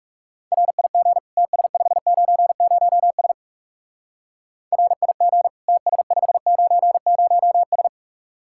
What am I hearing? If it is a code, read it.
Morse code, heard as a keyed tone, about 37 wpm, 700 Hz. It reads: RIG TS590S RIG TS590S
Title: Speed-Racing Common QSO Phrases - 30wpm